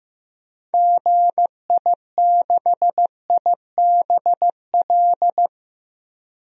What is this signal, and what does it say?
Morse code, heard as a keyed tone, about 15 wpm, 700 Hz. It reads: GI6IBL
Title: DXCC-100 Call Signs - 6 Characters - 15wpm